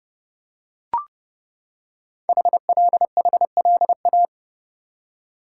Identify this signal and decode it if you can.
Morse code, heard as a keyed tone, about 31 wpm, 690 Hz. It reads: HLHLA
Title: Code Groups - 5 Letters - 30wpm